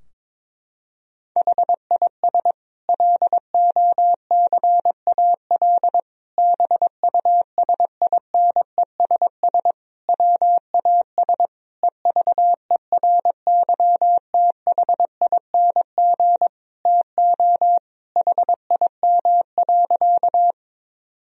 Morse code, heard as a keyed tone, about 22 wpm, 700 Hz. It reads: HIS LOCAL BUSINESS WAS EVERYTHING TO HIM.
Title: Sentences from Top 1000 Words - Review Part 3 - 22wpm